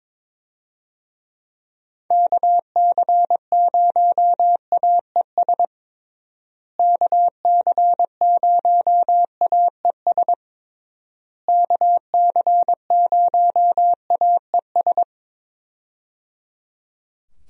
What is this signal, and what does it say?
Morse code, heard as a keyed tone, about 22 wpm, 700 Hz. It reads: KC0AES KC0AES KC0AES
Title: USA Call Signs - 6 Characters - Repeated 3x - 22wpm